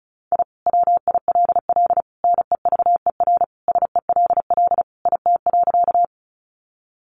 Morse code, heard as a keyed tone, about 34 wpm, 700 Hz. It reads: I WILL NEVER SELL IT.
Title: Sentences from Top 1100 Words 35wpm